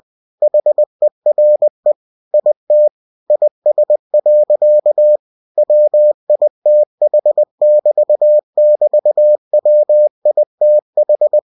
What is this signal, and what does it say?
Morse code, heard as a keyed tone, about 20 wpm, 600 Hz. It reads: HERE IT IS. WITH<BT><BT>WITH
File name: cw-books / winnie-the-pooh / winnie-20wpm-page-0056